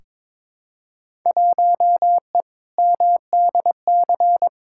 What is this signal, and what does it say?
Morse code, heard as a keyed tone, about 22 wpm, 700 Hz. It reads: 1E MDC
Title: Field Day Exchanges - 22wpm